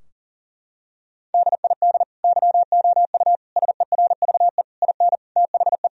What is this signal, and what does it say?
Morse code, heard as a keyed tone, about 40 wpm, 700 Hz. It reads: DID YOU SERVE IN THE
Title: Sentences from Top 1000 Words - Review Part 4 - 40wpm